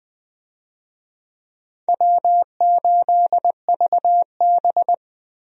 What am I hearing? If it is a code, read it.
Morse code, heard as a keyed tone, about 20 wpm, 700 Hz. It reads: W8VB
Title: USA Call Signs - 4 Characters - Repeated 3x - 20wpm